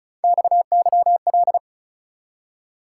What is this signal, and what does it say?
Morse code, heard as a keyed tone, about 45 wpm, 700 Hz. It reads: XYL